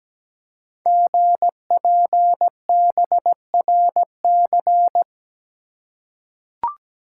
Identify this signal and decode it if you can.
Morse code, heard as a keyed tone, about 17 wpm, 700 Hz. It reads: GPBRC
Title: Code Groups - 5 Letters - 17wpm